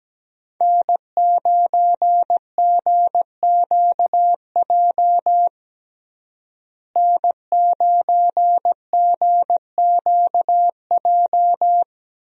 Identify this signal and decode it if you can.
Morse code, heard as a keyed tone, about 17 wpm, 700 Hz. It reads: N9GQJ N9GQJ
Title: USA Call Signs - 5 Characters - Repeated 3x - 17wpm